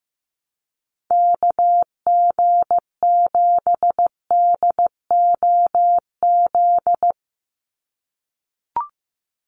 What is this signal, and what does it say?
Morse code, heard as a keyed tone, about 15 wpm, 700 Hz. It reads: KG7DOZ
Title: USA Call Signs 15wpm